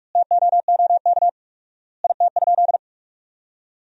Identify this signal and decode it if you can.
Morse code, heard as a keyed tone, about 45 wpm, 700 Hz. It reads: TOOK IT?
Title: 4x Word-Spacing - Sentences from Top 300 Words - 45wpm